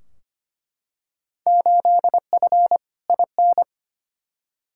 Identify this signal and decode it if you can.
Morse code, heard as a keyed tone, about 25 wpm, 700 Hz. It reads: 8F IN
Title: Field Day Exchanges Encore - 25wpm